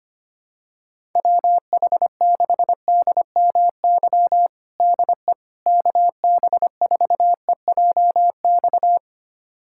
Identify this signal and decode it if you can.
Morse code, heard as a keyed tone, about 25 wpm, 700 Hz. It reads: WH6DMY DE KB4EJX
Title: USA Callsign DE Callsign - 6 Characters - 25wpm